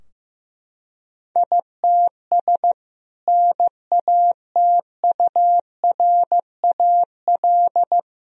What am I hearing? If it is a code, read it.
Morse code, heard as a keyed tone, about 15 wpm, 700 Hz. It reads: ITS NATURAL
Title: Sentences from Top 1000 Words - Review Part 4 - 15wpm